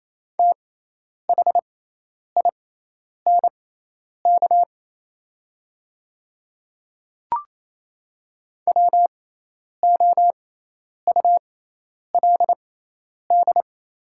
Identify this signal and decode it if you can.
Morse code, heard as a keyed tone, about 28 wpm, spaced as for 10 wpm, 700 Hz. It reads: THINK WOULD
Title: 6x Character Spacing - Top 200 Words - 28wpm